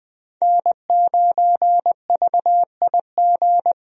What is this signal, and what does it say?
Morse code, heard as a keyed tone, about 20 wpm, 700 Hz. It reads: N9VIG
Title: USA Call Signs - 5 Characters - 20wpm